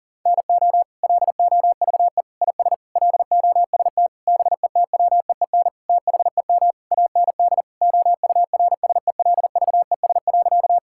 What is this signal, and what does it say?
Morse code, heard as a keyed tone, about 40 wpm, 700 Hz. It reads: NO LOVE IS LOST BETWEEN THEM AND OURSELVES.
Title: Sentences from Top 1600 Words 40wpm